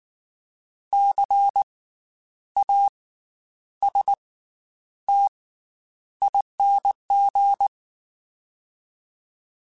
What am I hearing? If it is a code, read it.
Morse code, heard as a keyed tone, about 19 wpm, spaced as for 9 wpm, 800 Hz. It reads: CASTING
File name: Special INGs / Special INGs-mp3s / Slow ing @18